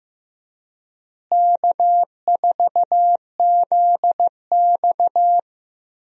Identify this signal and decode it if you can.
Morse code, heard as a keyed tone, about 15 wpm, 700 Hz. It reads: K4ZX
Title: USA Call Signs - 4 Characters - Repeated 3x - 15wpm